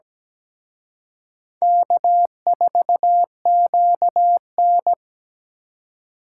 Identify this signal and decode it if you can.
Morse code, heard as a keyed tone, about 20 wpm, 700 Hz. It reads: K4QN
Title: Speed Racing - USA Call Signs 17wpm